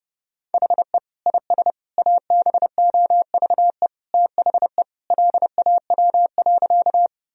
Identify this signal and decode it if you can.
Morse code, heard as a keyed tone, about 30 wpm, 700 Hz. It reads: HE IS ABOVE THE LAW.
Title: Sentences from Top 1000 Words - Review Part 3 - 30wpm